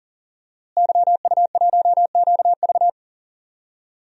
Morse code, heard as a keyed tone, about 37 wpm, 700 Hz. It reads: YU1QV
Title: DXCC-100 Call Signs - 5 Characters - Repeated 2x - 40wpm